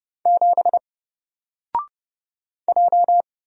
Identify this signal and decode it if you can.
Morse code, heard as a keyed tone, about 30 wpm, 700 Hz. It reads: 7 J
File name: Single-Letter-Number-30wpm